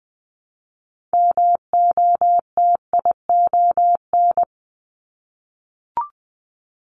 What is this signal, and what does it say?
Morse code, heard as a keyed tone, about 19 wpm, 700 Hz. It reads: MOTION
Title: Multi-letter Phonemes Repeated 3x 20wpm